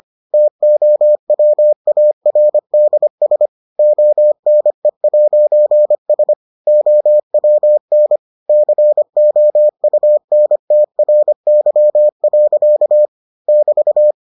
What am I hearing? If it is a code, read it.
Morse code, heard as a keyed tone, about 25 wpm, 600 Hz. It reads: TOWARDS ONE'S OWN COUNTRY. <BT>